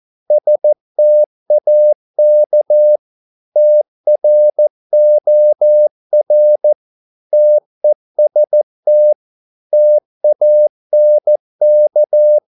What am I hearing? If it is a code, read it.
Morse code, heard as a keyed tone, about 14 wpm, 600 Hz. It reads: STAK TROR TEST TANK
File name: Korte ord DK 14wpm